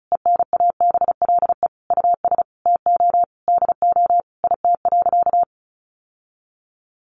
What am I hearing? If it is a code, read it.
Morse code, heard as a keyed tone, about 34 wpm, 700 Hz. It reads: ENABLE US TO DO IT.
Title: Sentences from Top 1500 Words 35wpm